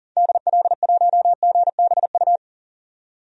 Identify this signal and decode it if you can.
Morse code, heard as a keyed tone, about 40 wpm, 700 Hz. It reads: DL1GBU
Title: DXCC-100 Call Signs - 6 Characters - 40wpm